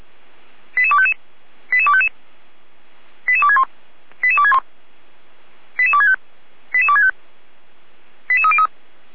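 Five-tone selective call (ZVEI), heard as quick runs of five tones, tones of about 70 ms. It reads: 88255 88255 88261 88261 88265 88265 88303
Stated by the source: ZVEI five-tone selective call received on an SDR.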